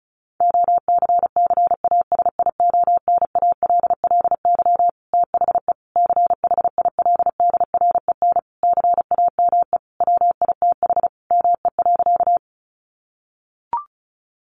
Morse code, heard as a keyed tone, about 34 wpm, 700 Hz. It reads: OCCASIONALLY THE CHILDREN CAME WITH ME.
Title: Sentences from Top 1300 Words 35wpm